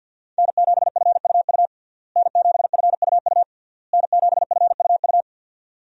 Morse code, heard as a keyed tone, about 47 wpm, 700 Hz. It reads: N7FUU N7FUU N7FUU
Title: USA Call Signs - 5 Characters - Repeated 3x - 50wpm